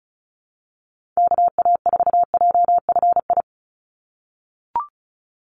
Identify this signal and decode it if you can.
Morse code, heard as a keyed tone, about 36 wpm, 700 Hz. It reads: KA4JFI
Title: USA Call Signs 35wpm